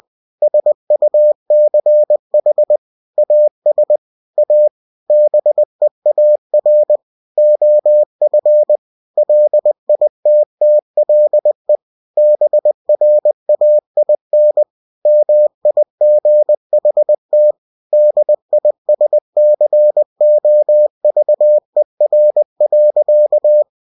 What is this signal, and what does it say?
Morse code, heard as a keyed tone, about 20 wpm, 600 Hz. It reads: SUCH AS A BEAR OF LITTLE BRAIN MIGHT DISCOVER.
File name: cw-books / winnie-the-pooh / winnie-20wpm-page-0093